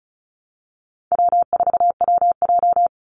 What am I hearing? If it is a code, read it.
Morse code, heard as a keyed tone, about 34 wpm, 700 Hz. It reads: W4WJ
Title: Easy Call Signs Repeated 2x - 35wpm